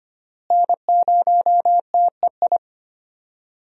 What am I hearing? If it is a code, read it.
Morse code, heard as a keyed tone, about 25 wpm, 700 Hz. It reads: N0TEI
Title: USA Call Signs - 5 Characters - Repeated 3x - 25wpm